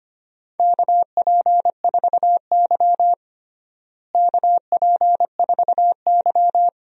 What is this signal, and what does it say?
Morse code, heard as a keyed tone, about 25 wpm, 700 Hz. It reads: KP4Y KP4Y
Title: DXCC-100 Call Signs - 4 Characters - Repeated 2x - 25wpm